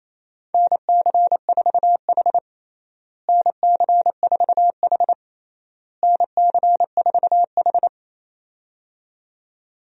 Morse code, heard as a keyed tone, about 27 wpm, 700 Hz. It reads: NC4H NC4H NC4H
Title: USA Call Signs - 4 Characters - Repeated 3x - 28wpm